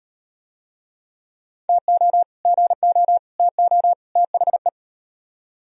Morse code, heard as a keyed tone, about 38 wpm, 700 Hz. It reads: TO GO TO THE